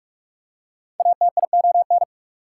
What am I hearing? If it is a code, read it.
Morse code, heard as a keyed tone, about 44 wpm, 690 Hz. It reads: ATION
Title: Prefixes and Suffixes 45wpm